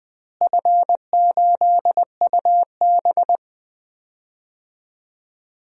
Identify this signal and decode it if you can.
Morse code, heard as a keyed tone, about 20 wpm, 700 Hz. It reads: F8UB
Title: DXCC-100 Call Signs - 4 Characters - Repeated 2x - 20wpm